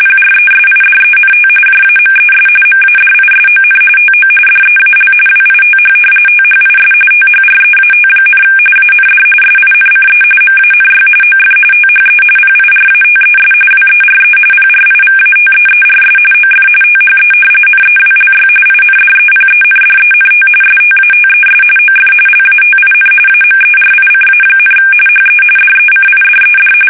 RTTY, about 45.45 baud. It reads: PRESIDENT JAMES MADISON TO DEAL WITH THIS THREAT, AND SINCE MANY OF THE PIRATES WERE PRIVATEERS OF THE NEWLY INDEPENDENT STATES OF LATIN AMERICA, HE DECIDED
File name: fsk850 / usnavy2